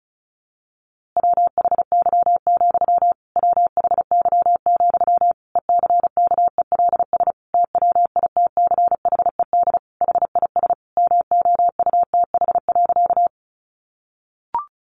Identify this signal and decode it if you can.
Morse code, heard as a keyed tone, about 34 wpm, 700 Hz. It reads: WHY, WHY, ECKELS TWITCHED HIS MOUTH.
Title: A Sound Of Thunder 35wpm